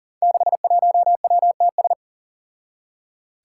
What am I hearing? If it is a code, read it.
Morse code, heard as a keyed tone, about 39 wpm, 700 Hz. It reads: 61WTS
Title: Five Challenging Letter-Number Combinations 40wpm